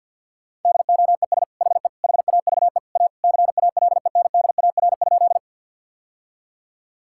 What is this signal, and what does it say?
Morse code, heard as a keyed tone, about 50 wpm, 700 Hz. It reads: DOES HE HAVE A CALENDAR?
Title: Sentences from Top 1600 Words 50wpm